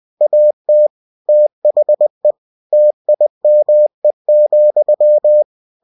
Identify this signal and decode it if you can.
Morse code, heard as a keyed tone, about 20 wpm, 600 Hz. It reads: AT THE TIME,
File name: cw-books / winnie-the-pooh / winnie-20wpm-page-0017